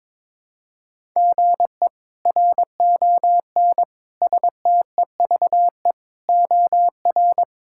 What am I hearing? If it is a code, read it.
Morse code, heard as a keyed tone, about 22 wpm, 700 Hz. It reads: GE RON STEVE OR
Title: K1USN SST QSO Practice - 22wpm